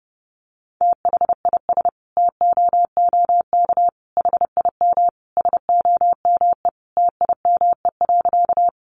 Morse code, heard as a keyed tone, about 30 wpm, 700 Hz. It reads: THIS TOOK HIM SOME TIME.
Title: Sentences from Top 300 Words 30wpm